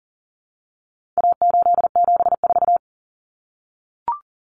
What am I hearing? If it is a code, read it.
Morse code, heard as a keyed tone, about 40 wpm, 700 Hz. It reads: A874
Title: Common Cut Numbers - 4-Digit 40wpm